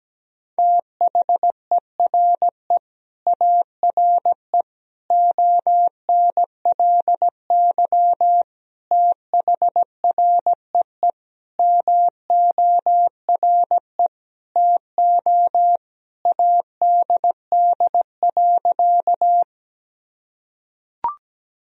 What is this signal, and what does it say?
Morse code, heard as a keyed tone, about 17 wpm, 700 Hz. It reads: THERE ARE ONLY THREE MORE TO ADD.
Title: Sentences from Top 1000 Words - Review Part 1 - 17wpm